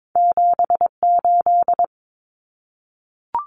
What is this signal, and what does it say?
Morse code, heard as a keyed tone, about 22 wpm, 700 Hz. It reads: 78
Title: Common Cut Numbers - 2-Digit 22wpm